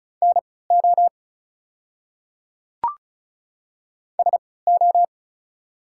Morse code, heard as a keyed tone, about 34 wpm, spaced as for 20 wpm, 690 Hz. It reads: NO SO
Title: 3x Character Spacing - Top 100 Words - 35wpm